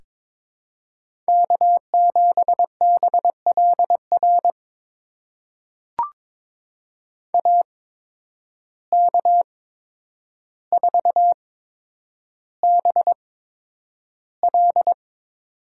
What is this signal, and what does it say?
Morse code, heard as a keyed tone, about 22 wpm, spaced as for 9 wpm, 700 Hz. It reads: K7BLR AK4BL
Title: USA Call Signs - 5 Characters - 8x Character Spacing - 22wpm